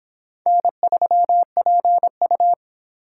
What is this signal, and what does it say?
Morse code, heard as a keyed tone, about 23 wpm, 700 Hz. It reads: N3PU